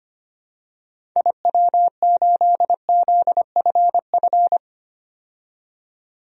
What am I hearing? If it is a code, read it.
Morse code, heard as a keyed tone, about 25 wpm, 700 Hz. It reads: IW8ZFF
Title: DXCC-100 Call Signs - 6 Characters - 25wpm